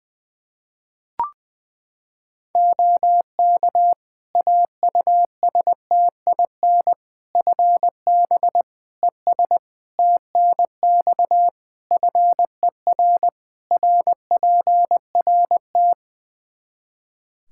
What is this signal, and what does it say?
Morse code, heard as a keyed tone, about 21 wpm, 700 Hz. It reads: OK AUSTIN FB ES TNX FER RPRT
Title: Practice QSO - Exchange 2 - Info - 20wpm